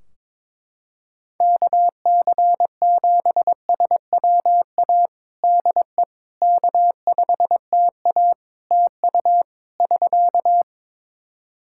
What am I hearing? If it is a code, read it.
Morse code, heard as a keyed tone, about 21 wpm, 700 Hz. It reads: KC7SWA DE K5TA TU <SK>
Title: Practice QSO - Exchange 4 - Ending - 22wpm